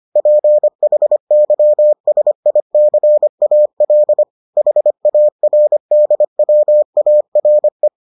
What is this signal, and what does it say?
Morse code, heard as a keyed tone, about 25 wpm, 600 Hz. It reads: PHYSICAL HARDWARE